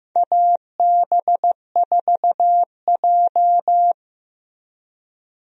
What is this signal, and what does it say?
Morse code, heard as a keyed tone, about 15 wpm, 700 Hz. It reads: AB4J
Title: USA Call Signs - 4 Characters - Repeated 2x - 15wpm